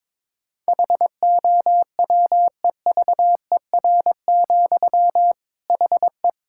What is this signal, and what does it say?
Morse code, heard as a keyed tone, about 21 wpm, 700 Hz. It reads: HOWEVER, HE
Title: Sentences from Top 1600 Words 22wpm